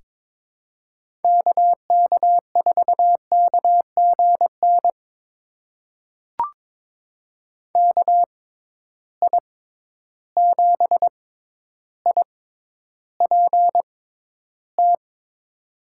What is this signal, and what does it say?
Morse code, heard as a keyed tone, about 22 wpm, spaced as for 11 wpm, 700 Hz. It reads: KK4KGN KI7IPT
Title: USA Call Signs - 6 Characters - 6x Character Spacing - 22wpm